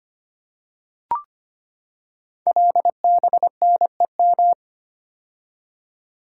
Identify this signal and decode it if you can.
Morse code, heard as a keyed tone, about 25 wpm, 700 Hz. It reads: LBNEM
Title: Code Groups - 5 Letters - 25wpm